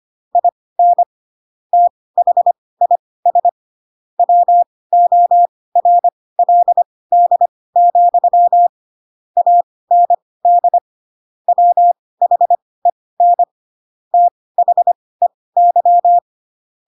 Morse code, heard as a keyed tone, about 25 wpm, spaced as for 18 wpm, 700 Hz. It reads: IN THIS WORLD, AND WHEN THEY
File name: Pinocchio / Pinocchio ch4 12 25 ik1hge